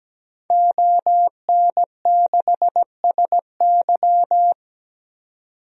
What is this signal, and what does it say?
Morse code, heard as a keyed tone, about 17 wpm, 700 Hz. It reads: ON6SY
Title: DXCC-100 Call Signs - 5 Characters - 17wpm